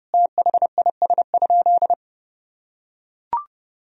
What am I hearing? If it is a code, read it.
Morse code, heard as a keyed tone, about 32 wpm, 700 Hz. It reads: THIS?
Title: Sentences from Top 1000 Words - Review Part 2 - 30wpm